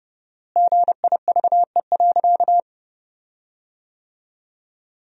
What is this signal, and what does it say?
Morse code, heard as a keyed tone, about 29 wpm, 700 Hz. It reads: GIVE.
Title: 4x Word-Spacing - Sentences from Top 200 Words - 30wpm